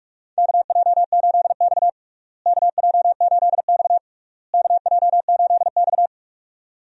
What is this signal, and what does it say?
Morse code, heard as a keyed tone, about 45 wpm, 700 Hz. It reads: KJ8X KJ8X KJ8X
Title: USA Call Signs - 4 Characters - Repeated 3x - 45wpm